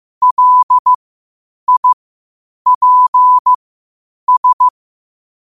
Morse code, heard as a keyed tone, about 15 wpm, spaced as for 8 wpm, 1000 Hz. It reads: LIPS
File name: qotdfiles / 8wpm